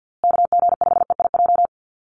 Morse code, heard as a keyed tone, about 50 wpm, 700 Hz. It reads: KG5EIJ
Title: USA Call Signs Repeated 3x 50wpm